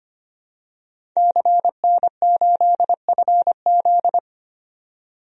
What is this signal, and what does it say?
Morse code, heard as a keyed tone, about 25 wpm, 700 Hz. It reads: CN8FZ
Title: DXCC-100 Call Signs - 5 Characters - 25wpm